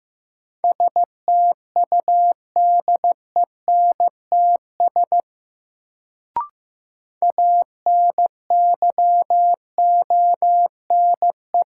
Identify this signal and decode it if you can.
Morse code, heard as a keyed tone, about 15 wpm, 700 Hz. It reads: STUDENTS ANYONE
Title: Sets of 5 Words Encore - New Word Drill - 15wpm